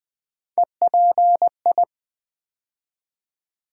Morse code, heard as a keyed tone, about 20 wpm, 700 Hz. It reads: EPI
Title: Build A Word - 3 Letters At A Time - 8-Letter Words - 20wpm